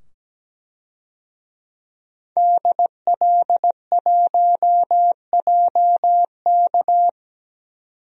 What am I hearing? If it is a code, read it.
Morse code, heard as a keyed tone, about 17 wpm, 700 Hz. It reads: DL1JK
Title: DXCC-100 Call Signs - 5 Characters - 17wpm